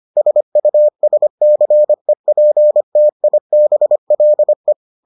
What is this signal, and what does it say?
Morse code, heard as a keyed tone, about 25 wpm, 600 Hz. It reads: SUSCEPTIBLE